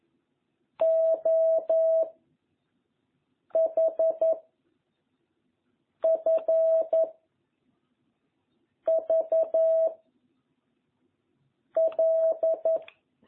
Morse code, heard as a keyed tone, about 11 wpm, 660 Hz. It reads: O H F V L